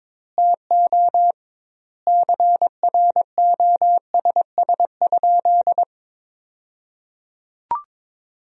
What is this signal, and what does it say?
Morse code, heard as a keyed tone, about 22 wpm, 700 Hz. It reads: TO CROSS?
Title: 2x Word-Spacing - Sentences from Top 300 Words - 22wpm